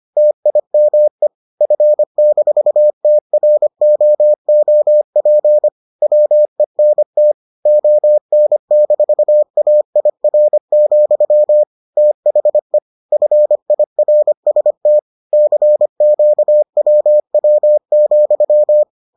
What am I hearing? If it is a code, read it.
Morse code, heard as a keyed tone, about 25 wpm, 600 Hz. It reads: TIME F-TROOP WENT ON-AIR, THE FIRST CQWW,